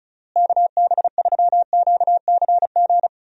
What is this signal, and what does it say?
Morse code, heard as a keyed tone, about 36 wpm, 700 Hz. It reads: KB3QCG
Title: USA Call Signs - 6 Characters - Repeated 3x - 35wpm